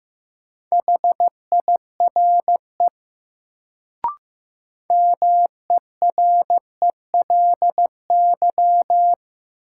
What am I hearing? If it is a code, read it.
Morse code, heard as a keyed tone, about 16 wpm, 700 Hz. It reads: HIRE MERELY
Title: Sentences from Top 1400 Words - New Word Drill Alt - 15wpm